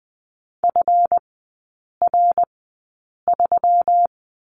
Morse code, heard as a keyed tone, about 20 wpm, spaced as for 8 wpm, 690 Hz. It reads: FR3
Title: 145 Lesson 24e - Letter y - Three Characters 20wpm